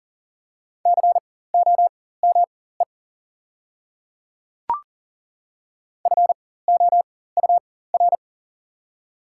Morse code, heard as a keyed tone, about 40 wpm, spaced as for 19 wpm, 700 Hz. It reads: COME FOUR
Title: 4x Character Spacing - Top 300 Words - 40wpm